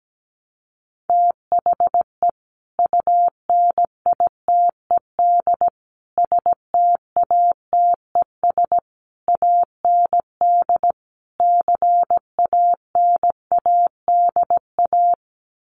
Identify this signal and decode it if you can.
Morse code, heard as a keyed tone, about 17 wpm, 700 Hz. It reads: THE UNITED STATES AND CANADA
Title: Sets of 5 Words 17wpm